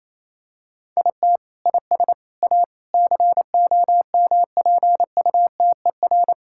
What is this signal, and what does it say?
Morse code, heard as a keyed tone, about 28 wpm, 700 Hz. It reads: IT IS A COMPUTER
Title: Sentences from Top 1000 Words - Review Part 6 - 28wpm